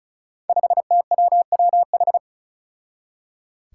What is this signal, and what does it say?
Morse code, heard as a keyed tone, about 36 wpm, 700 Hz. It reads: 5TWWH
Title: Five Challenging Letter-Number Combinations 35wpm